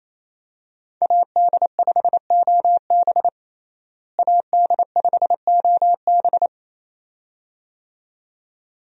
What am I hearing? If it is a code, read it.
Morse code, heard as a keyed tone, about 28 wpm, 700 Hz. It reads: AD5OB AD5OB
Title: USA Call Signs - 5 Characters - Repeated 2x - 28wpm